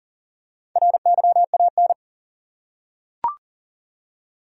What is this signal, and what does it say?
Morse code, heard as a keyed tone, about 40 wpm, 700 Hz. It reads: RYAN
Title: USA Common Names - Max 5 Letters - 40wpm